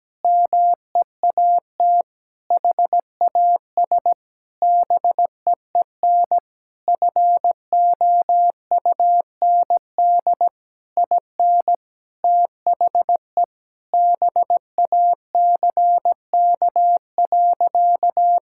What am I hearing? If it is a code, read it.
Morse code, heard as a keyed tone, about 17 wpm, 700 Hz. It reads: MEAT HAS BEEN FOUND IN THE BACK.
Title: Sentences from Top 1000 Words - Review Part 5 - 17wpm